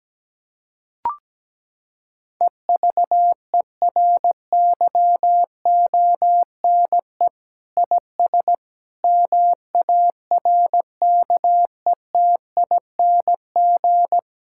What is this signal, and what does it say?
Morse code, heard as a keyed tone, about 17 wpm, 700 Hz. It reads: EVERYONE IS MARKETING
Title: Sentences from Top 1000 Words - Review Part 5 - 17wpm